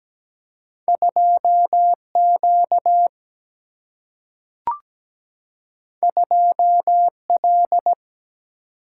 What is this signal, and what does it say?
Morse code, heard as a keyed tone, about 17 wpm, 700 Hz. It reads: 2Q 2L
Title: Single-Letter Single-Number 17wpm